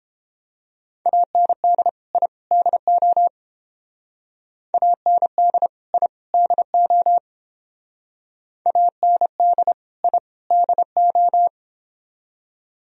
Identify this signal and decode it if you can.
Morse code, heard as a keyed tone, about 30 wpm, 700 Hz. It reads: AND I DO AND I DO AND I DO